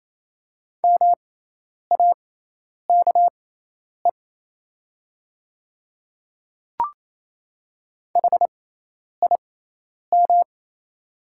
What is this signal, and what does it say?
Morse code, heard as a keyed tone, about 28 wpm, spaced as for 10 wpm, 700 Hz. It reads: MAKE HIM
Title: 6x Character Spacing - Top 100 Words - 28wpm